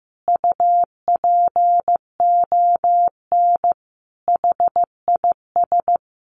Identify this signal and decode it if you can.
Morse code, heard as a keyed tone, about 15 wpm, 700 Hz. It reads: UPON HIS
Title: Sentences from Top 1100 Words 15wpm